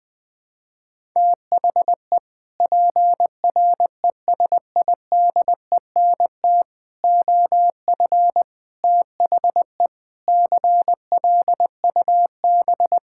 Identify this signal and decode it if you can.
Morse code, heard as a keyed tone, about 20 wpm, 700 Hz. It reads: THE PRESIDENT OF THE CLUB
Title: Sentences with 4-Word Phrases - Spoken Beginning - 20wpm